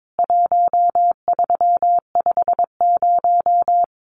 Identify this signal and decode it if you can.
Morse code, heard as a keyed tone, about 22 wpm, 700 Hz. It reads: 1350
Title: CWT Exchange 22wpm